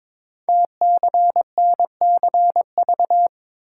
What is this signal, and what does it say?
Morse code, heard as a keyed tone, about 22 wpm, 700 Hz. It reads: TCNCV
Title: Code Groups - 5 Letters - 22wpm